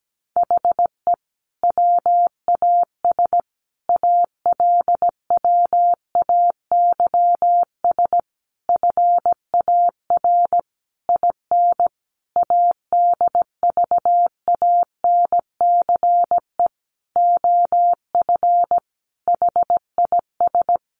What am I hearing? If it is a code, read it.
Morse code, heard as a keyed tone, about 17 wpm, 700 Hz. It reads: HE WAS ALWAYS FAR IN ADVANCE OF HIS
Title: Sentences from Top 1400 Words 17wpm